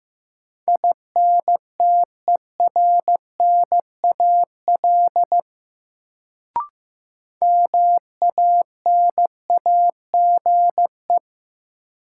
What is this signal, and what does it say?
Morse code, heard as a keyed tone, about 15 wpm, 700 Hz. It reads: INTERNAL MANAGE
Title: Sentences from Top 1400 Words - New Word Drill Alt - 15wpm